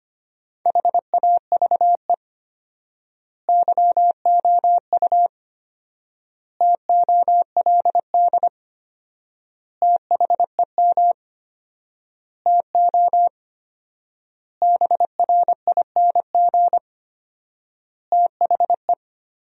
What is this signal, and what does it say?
Morse code, heard as a keyed tone, about 25 wpm, 700 Hz. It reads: HAVE YOU TOLD THEM TO BRING THE
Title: 4x Word-Spacing - Sentences from Top 500 Words - 25wpm